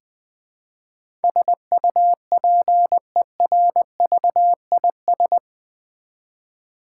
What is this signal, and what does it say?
Morse code, heard as a keyed tone, about 20 wpm, 700 Hz. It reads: SUPERVIS
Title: Build A Word - 4 Letters At A Time - 10-Letter Words - 20wpm